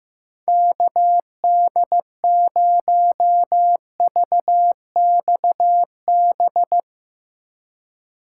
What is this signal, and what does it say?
Morse code, heard as a keyed tone, about 15 wpm, 700 Hz. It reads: KD0VXB
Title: USA Call Signs - 6 Characters - 2x Character Spacing - 15wpm